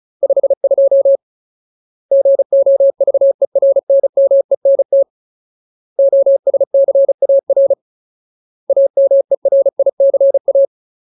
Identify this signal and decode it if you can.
Morse code, heard as a keyed tone, about 34 wpm, 550 Hz. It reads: 52 GOVERNMENT OSCAR AMERICA
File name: Ultimate Word List / 35WPM Ultimate